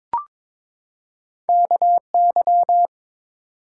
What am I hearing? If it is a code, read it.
Morse code, heard as a keyed tone, about 22 wpm, 700 Hz. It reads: KY
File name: ARRL Sections v2 - 22wpm